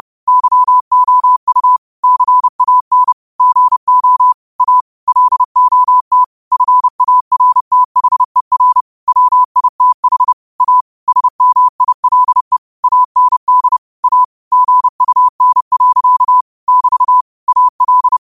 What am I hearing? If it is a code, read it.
Morse code, heard as a keyed tone, about 30 wpm, 1000 Hz. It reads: YOU CAN GO A LOT FARTHER WITH A SMILE AND A GUN. <BT> AL
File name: qotdfiles / 30wpm